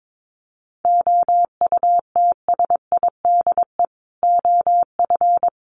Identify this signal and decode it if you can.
Morse code, heard as a keyed tone, about 22 wpm, 700 Hz. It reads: OUTSIDE OF
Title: Sentences from Top 1200 Words 22wpm